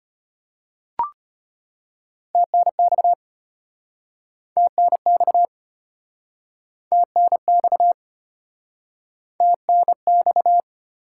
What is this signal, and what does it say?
Morse code, heard as a keyed tone, about 36 wpm, 700 Hz. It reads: TNX TNX TNX TNX